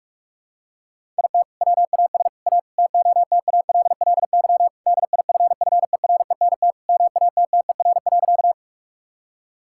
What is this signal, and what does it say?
Morse code, heard as a keyed tone, about 41 wpm, 700 Hz. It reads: IT WAS A TOTALLY DIFFERENT MATTER.
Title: Sentences from Top 1600 Words 45wpm